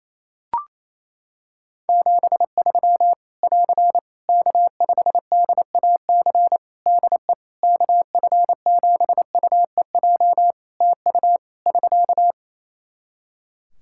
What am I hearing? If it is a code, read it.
Morse code, heard as a keyed tone, about 28 wpm, 700 Hz. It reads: 73 <AR> K5DAC DE KF7UEJ TU <SK>
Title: Practice QSO - Exchange 4 - Ending - 28wpm